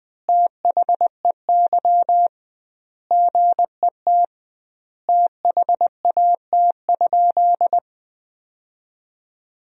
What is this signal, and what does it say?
Morse code, heard as a keyed tone, about 20 wpm, 700 Hz. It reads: THEY GET THAT?
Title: 2x Word-Spacing - Sentences from Top 200 Words - 20wpm